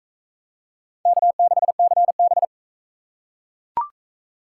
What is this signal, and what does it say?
Morse code, heard as a keyed tone, about 41 wpm, 700 Hz. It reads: K6CB